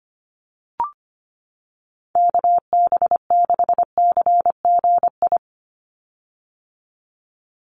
Morse code, heard as a keyed tone, about 25 wpm, 700 Hz. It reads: KB6CGI
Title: USA Call Signs 25wpm